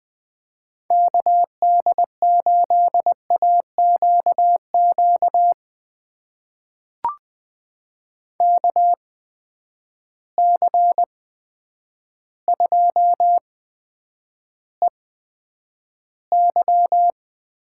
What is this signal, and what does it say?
Morse code, heard as a keyed tone, about 20 wpm, spaced as for 9 wpm, 700 Hz. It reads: KD8AQQ KC2EY
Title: USA Call Signs - 6 Characters - 8x Character Spacing - 20wpm